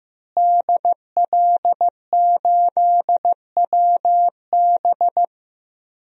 Morse code, heard as a keyed tone, about 15 wpm, 700 Hz. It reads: DL8WB
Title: DXCC-100 Call Signs - 5 Characters - Repeated 2x - 15wpm